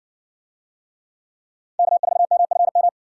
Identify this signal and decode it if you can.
Morse code, heard as a keyed tone, about 62 wpm, 700 Hz. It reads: K4MFM